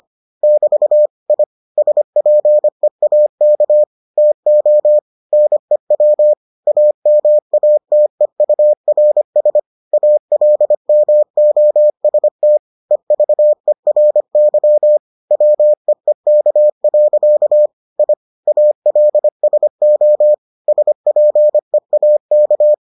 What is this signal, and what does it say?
Morse code, heard as a keyed tone, about 25 wpm, 600 Hz. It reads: <BT> I SPEAK TO NEW AMATEURS ALMOST EVERY WEEK. I ALSO SPEAK